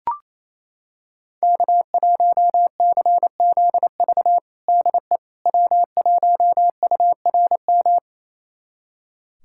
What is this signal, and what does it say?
Morse code, heard as a keyed tone, about 28 wpm, 700 Hz. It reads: K1CZV DE W1URM
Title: USA Callsign DE Callsign - 5 Characters - 28wpm